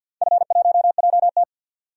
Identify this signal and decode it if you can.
Morse code, heard as a keyed tone, about 47 wpm, 700 Hz. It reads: F1JT
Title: DXCC-100 Call Signs - 4 Characters - 50wpm